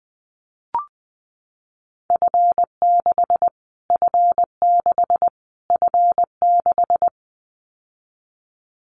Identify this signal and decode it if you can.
Morse code, heard as a keyed tone, about 20 wpm, 700 Hz. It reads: F6 F6 F6
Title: Callsign Prefixes for Top 25 DXCC - Repeated 3x - 20wpm